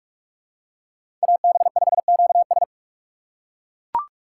Morse code, heard as a keyed tone, about 41 wpm, 700 Hz. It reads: AB5QS